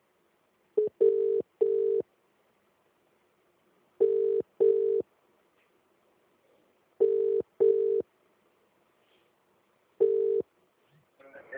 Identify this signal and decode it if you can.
Ringback tone (UK), 400 + 450 Hz.